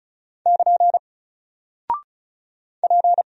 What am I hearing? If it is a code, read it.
Morse code, heard as a keyed tone, about 36 wpm, 700 Hz. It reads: <KN> P
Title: Single Character (Letter:Number:Punctuation:Prosign) Alt 35wpm